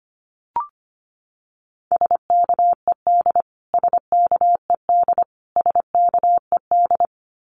Morse code, heard as a keyed tone, about 25 wpm, 700 Hz. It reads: SKED SKED SKED
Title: Abbreviations Repeated 3x - 25wpm